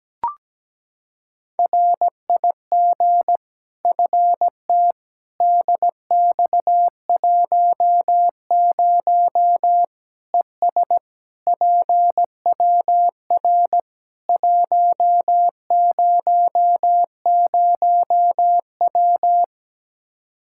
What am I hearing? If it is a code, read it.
Morse code, heard as a keyed tone, about 17 wpm, 700 Hz. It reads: RIG FT DX10 ES PWR 100W
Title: Practice QSO - Exchange 2 - Info - 17wpm